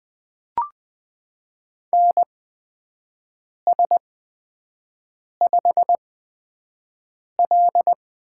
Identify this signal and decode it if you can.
Morse code, heard as a keyed tone, about 20 wpm, spaced as for 5 wpm, 700 Hz. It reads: NS5L
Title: USA Call Signs - 8x Character Spacing - 20wpm